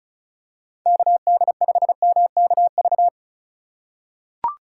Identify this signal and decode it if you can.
Morse code, heard as a keyed tone, about 36 wpm, 700 Hz. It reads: KD5MKV
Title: USA Call Signs - 6 Characters - 3x Character Spacing - 35wpm